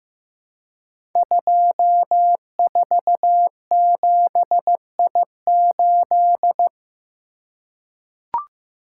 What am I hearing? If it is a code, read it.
Morse code, heard as a keyed tone, about 15 wpm, 700 Hz. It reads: 247I8
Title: Code Groups - 5 Letter:Numbers - 15wpm